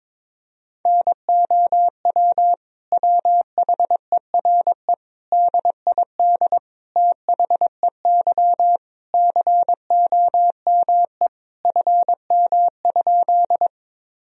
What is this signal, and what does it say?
Morse code, heard as a keyed tone, about 22 wpm, 700 Hz. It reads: NOW WHERE DID THEY COME FM?
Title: Sentences from Top 200 Words and Abbreviations 22wpm